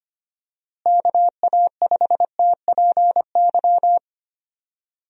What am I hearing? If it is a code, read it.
Morse code, heard as a keyed tone, about 25 wpm, 700 Hz. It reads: KA5TPY
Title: USA Call Signs - 6 Characters - Repeated 3x - 25wpm